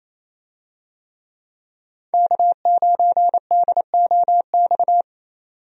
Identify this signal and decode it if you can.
Morse code, heard as a keyed tone, about 28 wpm, 700 Hz. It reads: K9DOX
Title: USA Call Signs - 5 Characters - 28wpm